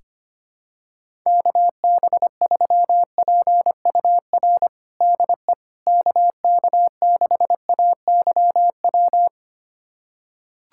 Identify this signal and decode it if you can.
Morse code, heard as a keyed tone, about 25 wpm, 700 Hz. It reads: KB3PUR DE KK6AYW
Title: USA Callsign DE Callsign - 6 Characters - 25wpm